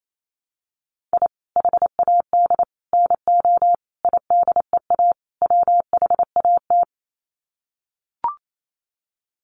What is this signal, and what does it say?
Morse code, heard as a keyed tone, about 28 wpm, 700 Hz. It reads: I HAD NO IDEA WHAT
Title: Sets of 5 Words Encore 28wpm